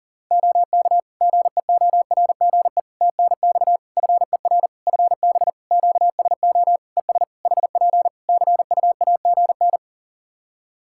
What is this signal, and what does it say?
Morse code, heard as a keyed tone, about 39 wpm, 700 Hz. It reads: OK GEORGE TNX FER FB QSO ES HP CUAGN
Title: K1USN SST QSO Practice 40wpm